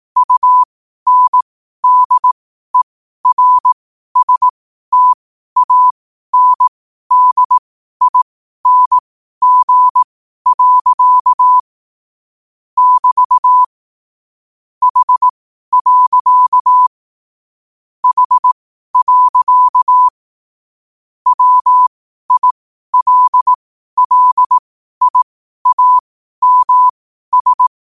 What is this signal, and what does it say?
Morse code, heard as a keyed tone, about 18 wpm, spaced as for 12 wpm, 1000 Hz. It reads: UNDERSTANDING. <BT> H. H. WILLIAMS
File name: qotdfiles / 12wpm